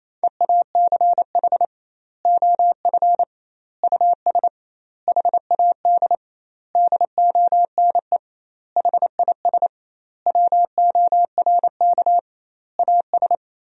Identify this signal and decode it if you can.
Morse code, heard as a keyed tone, about 28 wpm, 700 Hz. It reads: EACH OF US HAD DONE HIS WORK AS
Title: 2x Word-Spacing - Sentences from Top 500 Words - 28wpm